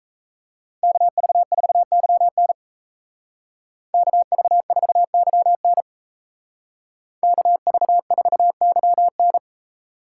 Morse code, heard as a keyed tone, about 33 wpm, 700 Hz. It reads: KV4YN KV4YN KV4YN
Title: Speed Racing - USA 2x2 Call Signs 28wpm